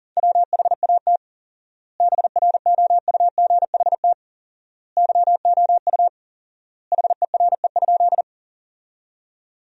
Morse code, heard as a keyed tone, about 40 wpm, 700 Hz. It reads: WHAT BROUGHT YOU HERE?
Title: 4x Word-Spacing - Sentences from Top 500 Words - 40wpm